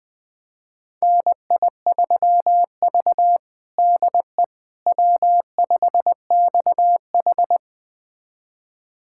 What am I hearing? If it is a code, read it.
Morse code, heard as a keyed tone, about 21 wpm, 700 Hz. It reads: NI3V DE W5XH
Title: USA Callsign DE Callsign - 4 Characters - 20wpm